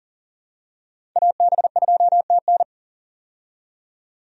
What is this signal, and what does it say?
Morse code, heard as a keyed tone, about 40 wpm, 700 Hz. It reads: AB2TN
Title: USA Call Signs - 5 Characters - 40wpm